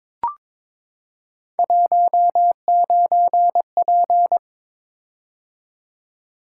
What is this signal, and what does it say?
Morse code, heard as a keyed tone, about 22 wpm, 700 Hz. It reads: 19P
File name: practice / 22wpm / Three Letter-Number Combinations - 22wpm